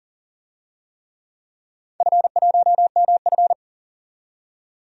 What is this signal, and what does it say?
Morse code, heard as a keyed tone, about 40 wpm, 700 Hz. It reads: F1MF
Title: DXCC-100 Call Signs - 4 Characters - 40wpm